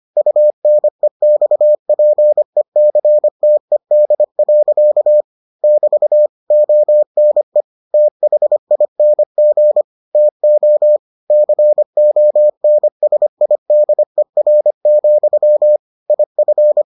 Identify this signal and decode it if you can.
Morse code, heard as a keyed tone, about 25 wpm, 600 Hz. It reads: UNEXPECTED. <BT> ONE THING TO CONSIDER, IF